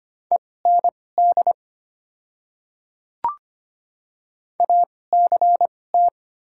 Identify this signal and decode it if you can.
Morse code, heard as a keyed tone, about 25 wpm, spaced as for 18 wpm, 700 Hz. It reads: END ACT
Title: 2x Character Spacing - Top 500 Words - 25wpm